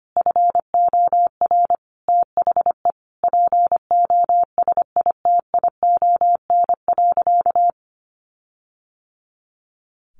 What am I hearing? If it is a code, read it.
Morse code, heard as a keyed tone, about 25 wpm, 700 Hz. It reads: FOR THE POSITION.
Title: Sentences from Top 1400 Words 25wpm